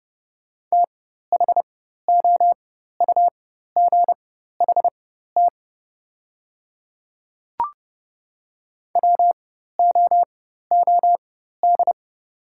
Morse code, heard as a keyed tone, about 30 wpm, spaced as for 14 wpm, 700 Hz. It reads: THOUGHT WOOD
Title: 4x Character Spacing - Top 300 Words - 30wpm